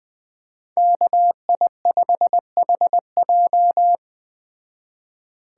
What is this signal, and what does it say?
Morse code, heard as a keyed tone, about 20 wpm, 700 Hz. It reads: KI5HJ